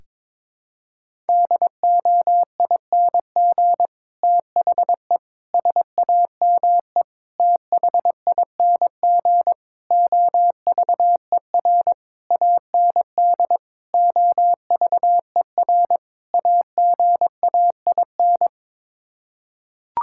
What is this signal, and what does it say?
Morse code, heard as a keyed tone, about 22 wpm, 700 Hz. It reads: DOING THE SAME THING OVER AND OVER AGAIN
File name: practice / 22wpm / Sets of 8 Words - 22wpm